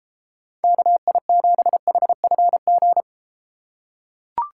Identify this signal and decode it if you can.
Morse code, heard as a keyed tone, about 33 wpm, 700 Hz. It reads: KI7HFG